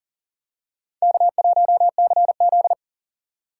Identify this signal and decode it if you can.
Morse code, heard as a keyed tone, about 40 wpm, 700 Hz. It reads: K1CZ
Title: USA Call Signs - 4 Characters - 4x Character Spacing - 40wpm